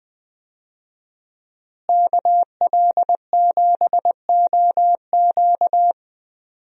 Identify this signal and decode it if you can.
Morse code, heard as a keyed tone, about 20 wpm, 700 Hz. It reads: KL7OQ
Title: DXCC-100 Call Signs - 5 Characters - Repeated 3x - 20wpm